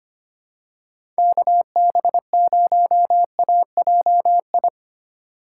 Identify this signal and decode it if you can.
Morse code, heard as a keyed tone, about 25 wpm, 700 Hz. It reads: KB0AJI
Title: USA Call Signs - 6 Characters - 2x Character Spacing - 25wpm